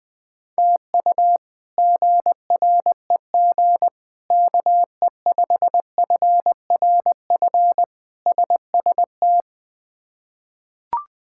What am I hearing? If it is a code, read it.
Morse code, heard as a keyed tone, about 21 wpm, 700 Hz. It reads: TU GREG KE5FRF SST
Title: K1USN SST QSO Practice - 20wpm